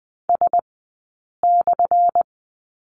Morse code, heard as a keyed tone, about 20 wpm, spaced as for 8 wpm, 700 Hz. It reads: S/
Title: 177 Lesson 29d - Slash - Two Characters 20wpm